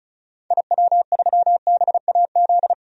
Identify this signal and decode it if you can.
Morse code, heard as a keyed tone, about 34 wpm, 700 Hz. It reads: IW3BAZ
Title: DXCC-100 Call Signs - 6 Characters - 35wpm